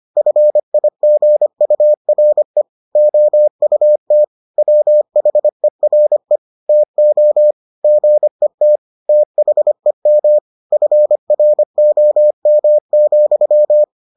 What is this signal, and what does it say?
Morse code, heard as a keyed tone, about 25 wpm, 600 Hz. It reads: FIGURE OUT WHERE TO GET THEM FROM,